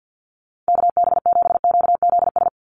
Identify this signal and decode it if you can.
Morse code, heard as a keyed tone, about 48 wpm, 700 Hz. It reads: KB7QZS
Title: USA Call Signs 50wpm